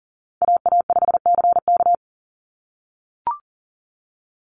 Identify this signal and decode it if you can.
Morse code, heard as a keyed tone, about 40 wpm, 700 Hz. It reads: AA5CK
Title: NA SOTA Chaser Callsigns 40wpm